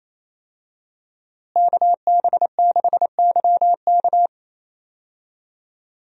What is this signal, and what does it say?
Morse code, heard as a keyed tone, about 28 wpm, 700 Hz. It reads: KB6YK
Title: USA Call Signs - 5 Characters - Repeated 3x - 28wpm